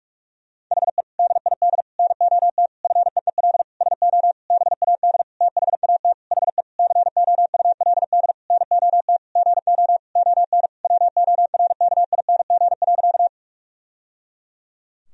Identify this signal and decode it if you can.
Morse code, heard as a keyed tone, about 41 wpm, 700 Hz. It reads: HE DID NOT FEEL SO BAD THAT HE COULD NOT GO ON WORKING.
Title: Sentences from Top 1000 Words - Review Part 4 - 45wpm